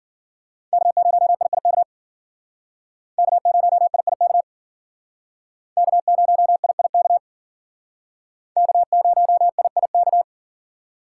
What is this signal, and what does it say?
Morse code, heard as a keyed tone, about 51 wpm, 700 Hz. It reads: K0IIK K0IIK K0IIK K0IIK